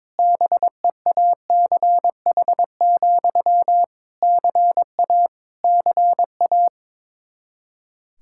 Morse code, heard as a keyed tone, about 22 wpm, 700 Hz. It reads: BEACH, CA CA
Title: Practice QSO - Exchange 1 - Beginning - 22wpm